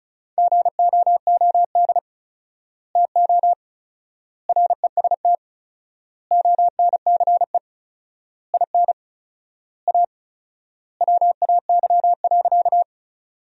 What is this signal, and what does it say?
Morse code, heard as a keyed tone, about 34 wpm, 700 Hz. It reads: GOOD TO REST ONCE IN A WAY.
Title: 4x Word-Spacing - Sentences from Top 500 Words - 35wpm